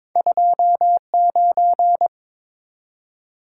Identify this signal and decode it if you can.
Morse code, heard as a keyed tone, about 22 wpm, 690 Hz. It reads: 29
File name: practice / 22wpm / Two Letter-Number Combinations - 22wpm